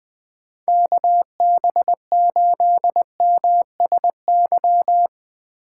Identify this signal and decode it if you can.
Morse code, heard as a keyed tone, about 20 wpm, 700 Hz. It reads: KB8MSY
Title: Speed Racing - USA Call Signs 17wpm